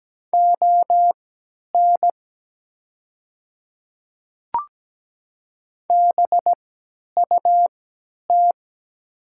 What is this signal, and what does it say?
Morse code, heard as a keyed tone, about 17 wpm, spaced as for 10 wpm, 700 Hz. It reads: ON BUT
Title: 3x Character Spacing - Top 300 Words - 17wpm